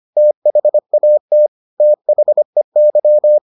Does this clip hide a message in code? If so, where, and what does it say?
Yes: Morse code from 0.2 to 3.4 s, heard as a keyed tone, about 25 wpm, 600 Hz. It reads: THAT THEY